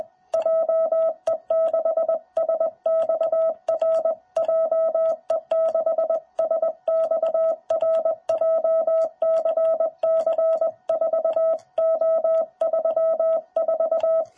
Morse code, heard as a keyed tone, about 20 wpm, 660 Hz. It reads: JE6SXR JE6SXR JCC4O34
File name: jh8cbh / 2021cw